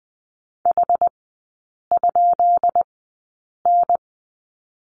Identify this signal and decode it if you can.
Morse code, heard as a keyed tone, about 20 wpm, spaced as for 8 wpm, 690 Hz. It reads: H?N
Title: 133 Lesson 22e - Question Mark - Three Characters 20wpm